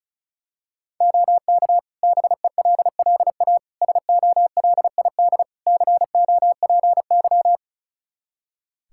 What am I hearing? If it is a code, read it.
Morse code, heard as a keyed tone, about 36 wpm, 700 Hz. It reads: OK BELLA SOLID COPY
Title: K1USN SST QSO Practice 35wpm